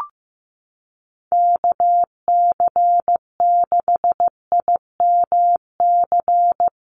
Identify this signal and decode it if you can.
Morse code, heard as a keyed tone, about 16 wpm, 700 Hz. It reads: KC6IMC
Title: USA Call Signs 15wpm